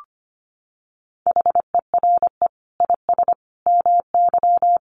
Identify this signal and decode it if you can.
Morse code, heard as a keyed tone, about 25 wpm, 700 Hz. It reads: HERE IS MY
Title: Sentences from Top 1400 Words 25wpm